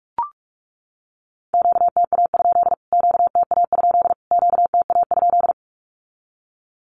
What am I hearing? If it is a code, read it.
Morse code, heard as a keyed tone, about 44 wpm, 700 Hz. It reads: QTA? QTA? QTA?
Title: Q-Codes Repeated 3x 45wpm